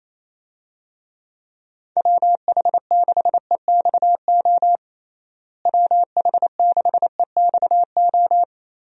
Morse code, heard as a keyed tone, about 28 wpm, 700 Hz. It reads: WH6EXO WH6EXO
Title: USA Call Signs - 6 Characters - Repeated 3x - 28wpm